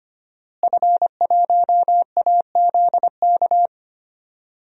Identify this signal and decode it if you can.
Morse code, heard as a keyed tone, about 25 wpm, 700 Hz. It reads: F1AZK
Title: DXCC-100 Call Signs - 5 Characters - Repeated 2x - 25wpm